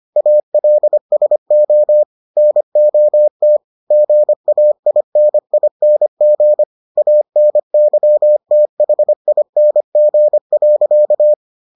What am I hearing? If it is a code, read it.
Morse code, heard as a keyed tone, about 25 wpm, 600 Hz. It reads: ALSO NOT GAINING ANYTHING.